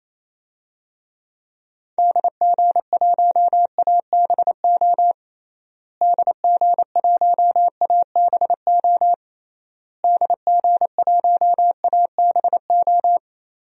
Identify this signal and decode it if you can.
Morse code, heard as a keyed tone, about 28 wpm, 700 Hz. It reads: DG1ABO DG1ABO DG1ABO
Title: DXCC-100 Call Signs - 6 Characters - Repeated 3x - 28wpm